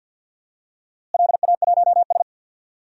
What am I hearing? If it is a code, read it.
Morse code, heard as a keyed tone, about 53 wpm, 700 Hz. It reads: LA1S
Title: DXCC-100 Call Signs - 4 Characters - Repeated 2x - 50wpm